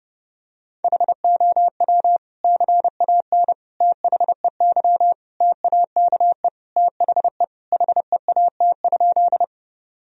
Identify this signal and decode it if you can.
Morse code, heard as a keyed tone, about 30 wpm, 700 Hz. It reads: HOW CAN THEY TAKE THE HEAT?
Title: Sentences from Top 1000 Words - Review Part 3 - 30wpm